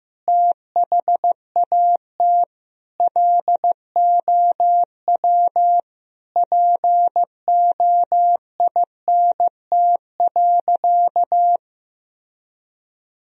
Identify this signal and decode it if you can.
Morse code, heard as a keyed tone, about 15 wpm, 700 Hz. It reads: THAT LOW POINT.
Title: Sentences from Top 1000 Words - Review Part 1 - 15wpm